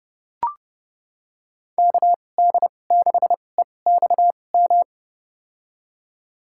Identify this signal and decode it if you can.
Morse code, heard as a keyed tone, about 32 wpm, spaced as for 22 wpm, 700 Hz. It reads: KD6EXM
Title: USA Call Signs - 6 Characters - 2x Character Spacing - 30wpm